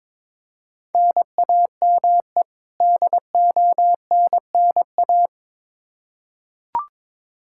Morse code, heard as a keyed tone, about 22 wpm, 700 Hz. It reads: NAME DONNA
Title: Practice QSO - Exchange 1 - Beginning - 22wpm